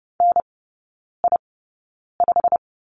Morse code, heard as a keyed tone, about 30 wpm, spaced as for 9 wpm, 700 Hz. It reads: NI5
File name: course / 30:10wpm / 087 Lesson 15e - Number 5 - Three Characters 30wpm